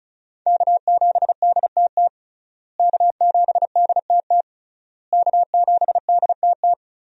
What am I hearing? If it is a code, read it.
Morse code, heard as a keyed tone, about 36 wpm, 700 Hz. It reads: K7DTT K7DTT K7DTT
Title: USA Call Signs - 5 Characters - Repeated 3x - 35wpm